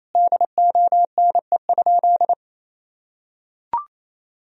Morse code, heard as a keyed tone, about 28 wpm, 700 Hz. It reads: DONE?
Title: Sentences from Top 500 Words and Abbreviations 28wpm